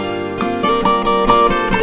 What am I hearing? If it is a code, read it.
There is no code.